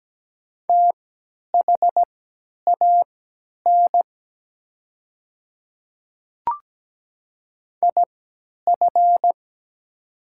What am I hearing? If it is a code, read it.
Morse code, heard as a keyed tone, about 17 wpm, spaced as for 10 wpm, 700 Hz. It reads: THAN IF
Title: 3x Character Spacing - Top 100 Words - 17wpm